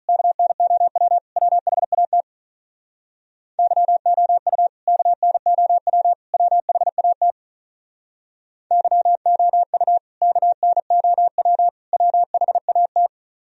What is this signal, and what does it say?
Morse code, heard as a keyed tone, about 42 wpm, 690 Hz. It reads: KNOW WHAT YOU KNOW WHAT YOU KNOW WHAT
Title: Speed-Racing - Sets of 3 Words 35wpm